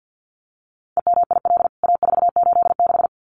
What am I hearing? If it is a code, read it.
Morse code, heard as a keyed tone, about 50 wpm, 700 Hz. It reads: EMIL A486
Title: CWT Exchange 50wpm